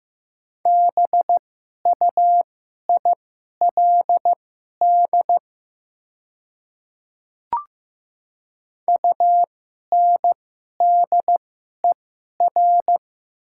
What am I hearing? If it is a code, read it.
Morse code, heard as a keyed tone, about 15 wpm, spaced as for 11 wpm, 700 Hz. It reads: BUILD UNDER
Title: 2x Character Spacing - Top 200 Words - 15wpm